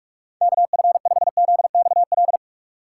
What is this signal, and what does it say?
Morse code, heard as a keyed tone, about 47 wpm, 700 Hz. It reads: KF5ZXL
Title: USA Call Signs - 6 Characters - 4x Character Spacing - 45wpm